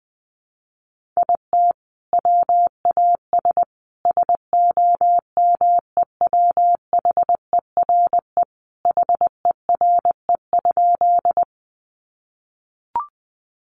Morse code, heard as a keyed tone, about 20 wpm, 700 Hz. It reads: IT WAS SOMEWHERE HERE?
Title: Sentences from Top 1400 Words 20wpm